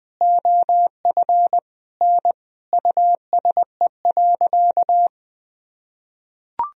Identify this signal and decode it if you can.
Morse code, heard as a keyed tone, about 20 wpm, 700 Hz. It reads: OF N USE.
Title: Sentences from Top 100 Words and Abbreviations 20wpm